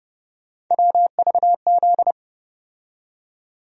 Morse code, heard as a keyed tone, about 30 wpm, 700 Hz. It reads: WVZ
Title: USA Call Sign Postfixes - 3 Letters 30wpm